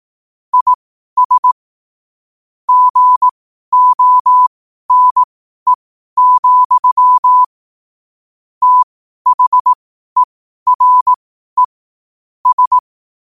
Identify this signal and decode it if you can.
Morse code, heard as a keyed tone, about 18 wpm, spaced as for 12 wpm, 1000 Hz. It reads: IS GONE, THERE S
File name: qotdfiles / 12wpm